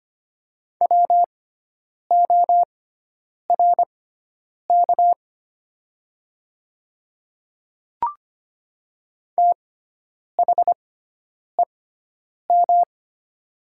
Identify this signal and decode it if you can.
Morse code, heard as a keyed tone, about 25 wpm, spaced as for 9 wpm, 700 Hz. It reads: WORK THEM
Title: 6x Character Spacing - Top 300 Words - 25wpm